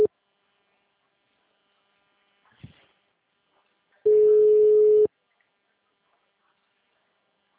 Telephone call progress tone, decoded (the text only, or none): ringback tone